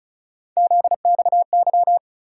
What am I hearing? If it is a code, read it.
Morse code, heard as a keyed tone, about 36 wpm, 700 Hz. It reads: ZXY
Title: USA Call Sign Postfixes - 3 Letters 35wpm